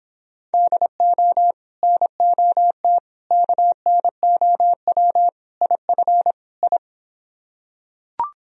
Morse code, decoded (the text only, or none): DO NOT KNOW IF I